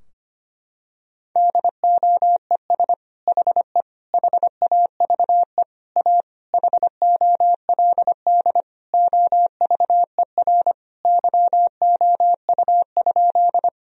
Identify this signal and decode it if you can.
Morse code, heard as a keyed tone, about 25 wpm, 700 Hz. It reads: DOES HE HAVE A HOLD OVER YOU?
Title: Sentences from Top 1000 Words - Review Part 2 - 25wpm